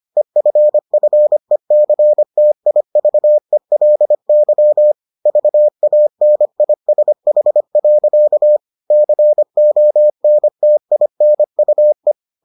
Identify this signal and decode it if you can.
Morse code, heard as a keyed tone, about 25 wpm, 600 Hz. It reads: EFFECTIVELY VANISH. CONTINUE